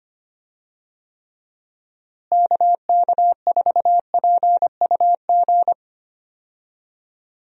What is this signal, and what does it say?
Morse code, heard as a keyed tone, about 25 wpm, 700 Hz. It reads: KK4PUG
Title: USA Call Signs - 6 Characters - Repeated 2x - 25wpm